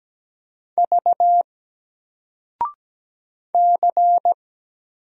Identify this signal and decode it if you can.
Morse code, heard as a keyed tone, about 17 wpm, 700 Hz. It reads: V C
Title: Single Letter-Number 17wpm